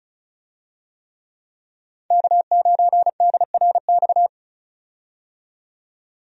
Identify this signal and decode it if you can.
Morse code, heard as a keyed tone, about 36 wpm, 700 Hz. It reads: K9DRX
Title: USA Call Signs - 5 Characters - 35wpm